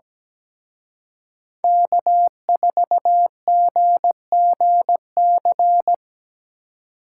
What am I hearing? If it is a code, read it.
Morse code, heard as a keyed tone, about 20 wpm, 700 Hz. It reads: K4GGC
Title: Speed Racing - USA 1x3 Call Signs 17wpm